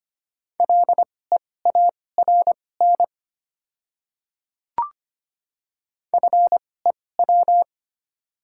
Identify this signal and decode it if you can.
Morse code, heard as a keyed tone, about 25 wpm, spaced as for 18 wpm, 690 Hz. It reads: LEARN FEW
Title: 2x Character Spacing - Top 300 Words - 25wpm